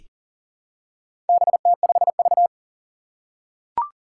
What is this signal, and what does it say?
Morse code, heard as a keyed tone, about 38 wpm, 700 Hz. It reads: BT5V
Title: Four Challenging Letter-Number Combinations 40wpm